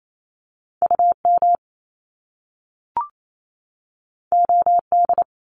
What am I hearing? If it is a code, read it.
Morse code, heard as a keyed tone, about 28 wpm, 700 Hz. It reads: U M O D
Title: Top Two Letter Combinations in Words 28wpm